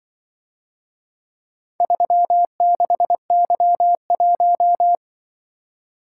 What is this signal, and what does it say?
Morse code, heard as a keyed tone, about 24 wpm, 700 Hz. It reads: 36Y1
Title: Speed-Racing - Four Letter-Number Combinations - 20wpm